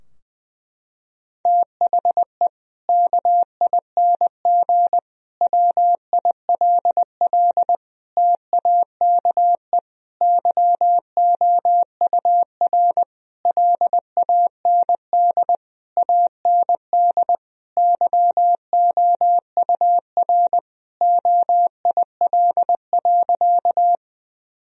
Morse code, heard as a keyed tone, about 21 wpm, 700 Hz. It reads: THE KING WILL TAKE YOUR LAND AND YOUR OIL.
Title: Sentences from Top 1000 Words - Review Part 4 - 20wpm